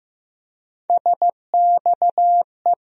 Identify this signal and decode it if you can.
Morse code, heard as a keyed tone, about 15 wpm, 700 Hz. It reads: SXE
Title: USA Call Sign Postfixes - 3 Letters 15wpm